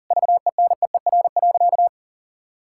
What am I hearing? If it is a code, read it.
Morse code, heard as a keyed tone, about 40 wpm, 700 Hz. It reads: VENEER.
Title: Pangram Challenge v2 - 40wpm